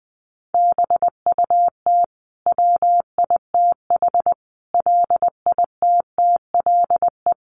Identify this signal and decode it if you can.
Morse code, heard as a keyed tone, about 20 wpm, 700 Hz. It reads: BUT WITH LITTLE
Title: Sentences from Top 1400 Words 20wpm